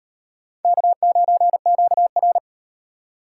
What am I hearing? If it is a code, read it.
Morse code, heard as a keyed tone, about 38 wpm, 700 Hz. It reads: K9QR